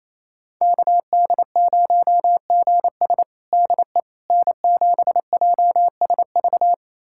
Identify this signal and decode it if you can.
Morse code, heard as a keyed tone, about 28 wpm, 700 Hz. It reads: KD0GS DE N7JSV
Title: USA Callsign DE Callsign - 5 Characters - 28wpm